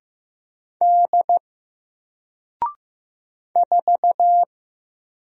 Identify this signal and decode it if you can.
Morse code, heard as a keyed tone, about 15 wpm, 700 Hz. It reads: D 4
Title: Single Letter-Number 15wpm